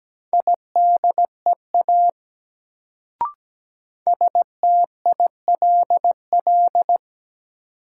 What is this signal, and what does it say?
Morse code, heard as a keyed tone, about 17 wpm, 700 Hz. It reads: IDEA STILL
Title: Sentences from Top 300 Words - New Word Drill - 17wpm